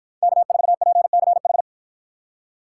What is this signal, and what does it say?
Morse code, heard as a keyed tone, about 49 wpm, 700 Hz. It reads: K4PCH